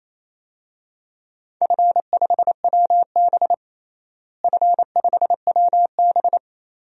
Morse code, heard as a keyed tone, about 28 wpm, 700 Hz. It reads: F5WB F5WB
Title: DXCC-100 Call Signs - 4 Characters - Repeated 2x - 28wpm